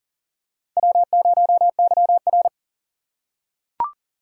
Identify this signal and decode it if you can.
Morse code, heard as a keyed tone, about 39 wpm, 700 Hz. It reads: W0YR
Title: K1USN SST QSO Practice - 40wpm